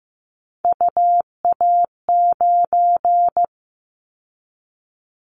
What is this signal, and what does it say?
Morse code, heard as a keyed tone, about 15 wpm, 700 Hz. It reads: UA9
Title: Callsign Prefixes for Top 25 DXCC - 15wpm